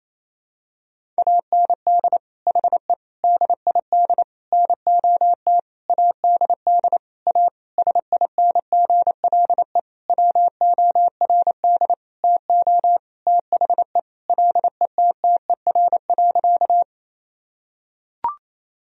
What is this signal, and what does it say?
Morse code, heard as a keyed tone, about 28 wpm, 700 Hz. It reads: AND HE DID NOT ADD A SINGLE WORD TO THE LETTER.
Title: Sentences from Top 1000 Words - Review Part 1 - 28wpm